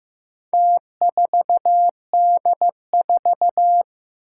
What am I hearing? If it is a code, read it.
Morse code, heard as a keyed tone, about 15 wpm, 700 Hz. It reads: T4D4
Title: Four Challenging Letter-Number Combinations 15wpm